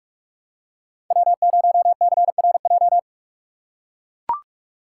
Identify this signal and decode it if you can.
Morse code, heard as a keyed tone, about 45 wpm, 700 Hz. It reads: W0CRJ
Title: USA Call Signs - 6x Character Spacing - 45wpm